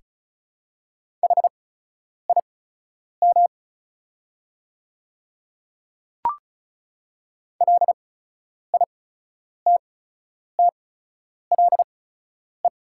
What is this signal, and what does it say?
Morse code, heard as a keyed tone, about 34 wpm, spaced as for 10 wpm, 690 Hz. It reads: HIM LITTLE
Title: 8x Character Spacing - Top 200 Words - 35wpm